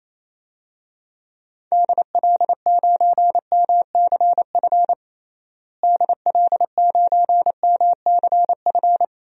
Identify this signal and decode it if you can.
Morse code, heard as a keyed tone, about 28 wpm, 700 Hz. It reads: DL9MCF DL9MCF
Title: DXCC-100 Call Signs - 6 Characters - Repeated 3x - 28wpm